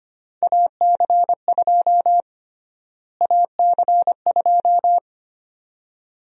Morse code, heard as a keyed tone, about 25 wpm, 700 Hz. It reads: AC2 AC2
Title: USA Call Sign Prefixes - Repeated 2x - 25wpm